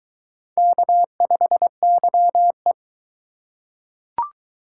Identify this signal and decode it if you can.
Morse code, heard as a keyed tone, about 21 wpm, 700 Hz. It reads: K5YE